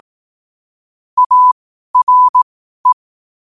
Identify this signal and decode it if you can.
Morse code, heard as a keyed tone, about 18 wpm, spaced as for 12 wpm, 1000 Hz. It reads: ARE
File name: qotdfiles / 12wpm